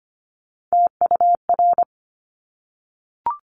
Morse code, heard as a keyed tone, about 25 wpm, 700 Hz. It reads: TUR
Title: Top Three Letter Combinations in Words 25wpm